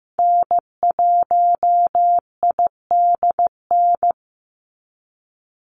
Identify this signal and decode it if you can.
Morse code, heard as a keyed tone, about 15 wpm, 700 Hz. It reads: N1IDN
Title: NA SOTA Activator Callsigns 15wpm